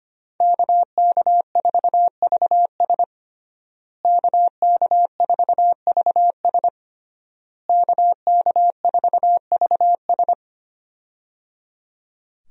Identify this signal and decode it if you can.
Morse code, heard as a keyed tone, about 25 wpm, 700 Hz. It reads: KK4VS KK4VS KK4VS
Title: USA Call Signs - 5 Characters - Repeated 3x - 25wpm